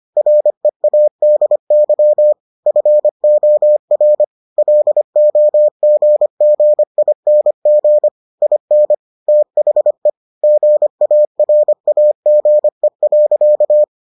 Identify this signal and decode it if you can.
Morse code, heard as a keyed tone, about 25 wpm, 600 Hz. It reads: READY FOR LOGGING IN THE GARAGE.